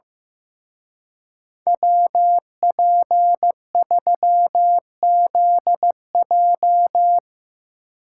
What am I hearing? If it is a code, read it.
Morse code, heard as a keyed tone, about 16 wpm, 700 Hz. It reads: WP3ZJ
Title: Speed Racing - USA 2x2 Call Signs 15wpm